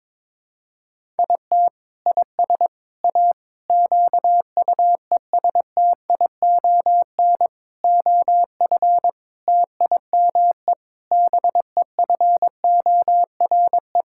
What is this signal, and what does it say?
Morse code, heard as a keyed tone, about 22 wpm, 700 Hz. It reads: IT IS A QUESTION OF TIME BEFORE
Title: Sentences from Top 1000 Words - Review Part 6 - 22wpm